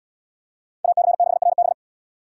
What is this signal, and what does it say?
Morse code, heard as a keyed tone, about 71 wpm, 700 Hz. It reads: AC6RB